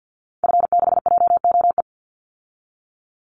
Morse code, heard as a keyed tone, about 50 wpm, 700 Hz. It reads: F6JOE
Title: Call Signs - 50wpm